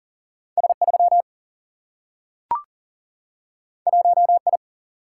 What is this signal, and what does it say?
Morse code, heard as a keyed tone, about 40 wpm, 700 Hz. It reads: S3 1I
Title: Single-Letter Single-Number 40wpm